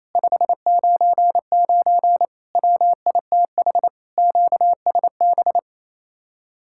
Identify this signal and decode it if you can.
Morse code, heard as a keyed tone, about 36 wpm, 700 Hz. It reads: 599 WITH QSB